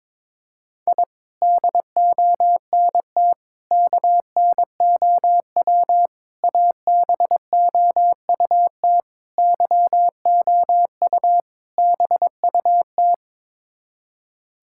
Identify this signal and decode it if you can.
Morse code, heard as a keyed tone, about 22 wpm, 700 Hz. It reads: I DONT KNOW ABOUT YOU BUT
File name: practice / 22wpm / Sets of 6 Words - 22wpm